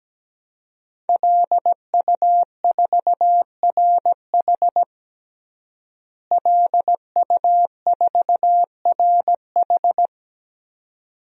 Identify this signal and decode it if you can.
Morse code, heard as a keyed tone, about 17 wpm, 700 Hz. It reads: LU4RH LU4RH
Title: DXCC-100 Call Signs - 5 Characters - Repeated 2x - 17wpm